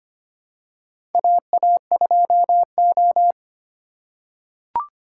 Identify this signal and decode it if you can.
Morse code, heard as a keyed tone, about 25 wpm, 700 Hz. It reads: AA2O
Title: USA Call Signs - 4 Characters - 3x Character Spacing - 25wpm